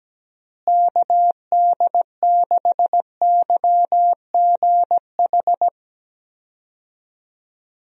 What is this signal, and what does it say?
Morse code, heard as a keyed tone, about 17 wpm, 700 Hz. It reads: KD6YGH
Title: USA Call Signs - 6 Characters - Repeated 2x - 17wpm